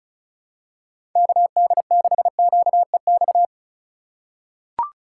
Morse code, heard as a keyed tone, about 36 wpm, 700 Hz. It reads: KD6QEX
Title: USA Call Signs - 6 Characters - 3x Character Spacing - 35wpm